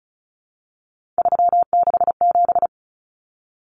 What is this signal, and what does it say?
Morse code, heard as a keyed tone, about 34 wpm, 700 Hz. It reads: 367
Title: Common Cut Numbers - 3-Digit 35wpm